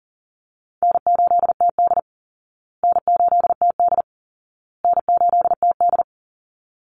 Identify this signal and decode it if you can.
Morse code, heard as a keyed tone, about 37 wpm, 700 Hz. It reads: N8TD N8TD N8TD
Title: Easy Call Signs Repeated 3x - 40wpm